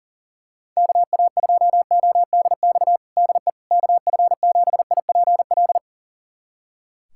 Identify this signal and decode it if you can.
Morse code, heard as a keyed tone, about 37 wpm, 700 Hz. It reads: KA2ODX DE KF7IPL
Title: USA Callsign DE Callsign - 6 Characters - 40wpm